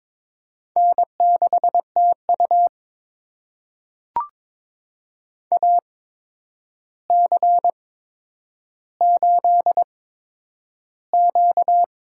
Morse code, heard as a keyed tone, about 22 wpm, spaced as for 9 wpm, 700 Hz. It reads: N6TU AC8Q
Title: USA Call Signs - 4 Characters - 8x Character Spacing - 22wpm